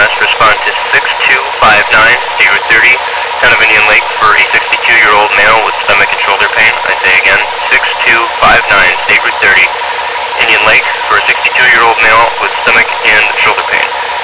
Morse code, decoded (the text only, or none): none